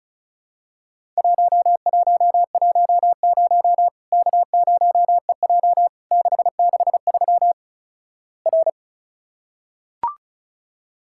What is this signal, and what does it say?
Morse code, heard as a keyed tone, about 36 wpm, 700 Hz. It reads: 1110 K0EJ 663 R
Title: Worked All Europe DX Contest - QTC Exchange - 35wpm